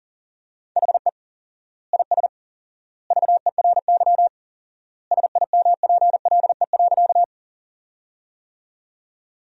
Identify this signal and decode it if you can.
Morse code, heard as a keyed tone, about 37 wpm, 700 Hz. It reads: HE IS VERY SIMPLE.
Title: 4x Word-Spacing - Sentences from Top 400 Words - 40wpm